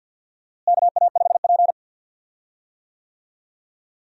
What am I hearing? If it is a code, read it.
Morse code, heard as a keyed tone, about 53 wpm, 700 Hz. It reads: KA5P
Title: USA Call Signs - 4 Characters - Repeated 2x - 50wpm